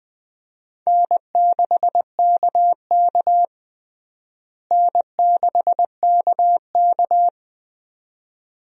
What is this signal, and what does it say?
Morse code, heard as a keyed tone, about 20 wpm, 700 Hz. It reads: N6KK N6KK
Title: USA Call Signs - 4 Characters - Repeated 2x - 20wpm